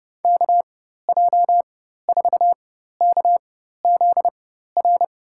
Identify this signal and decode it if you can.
Morse code, heard as a keyed tone, about 30 wpm, spaced as for 14 wpm, 700 Hz. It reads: KJ4KZR
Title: USA Call Signs - 6 Characters - 4x Character Spacing - 30wpm